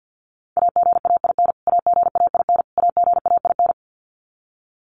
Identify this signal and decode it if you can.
Morse code, heard as a keyed tone, about 50 wpm, 700 Hz. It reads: AGAIN AGAIN AGAIN
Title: Top 500 Words Repeated 50wpm